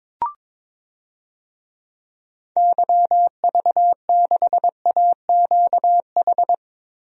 Morse code, heard as a keyed tone, about 22 wpm, 700 Hz. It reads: YV6AQH
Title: DXCC-100 Call Signs - 6 Characters - Repeated 2x - 22wpm